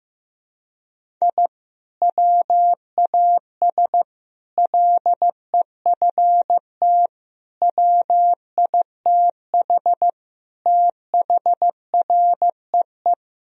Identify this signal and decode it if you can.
Morse code, heard as a keyed tone, about 16 wpm, 700 Hz. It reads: I WAS LEFT WITH THREE
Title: Sentences from Top 1000 Words - Review Part 1 - 15wpm